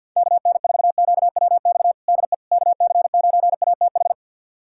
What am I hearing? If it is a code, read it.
Morse code, heard as a keyed tone, about 51 wpm, 700 Hz. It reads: KN4QWX DE KC9ATH
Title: USA Callsign DE Callsign - 6 Characters - 50wpm